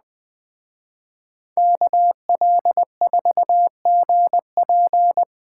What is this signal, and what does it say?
Morse code, heard as a keyed tone, about 21 wpm, 700 Hz. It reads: KL4GP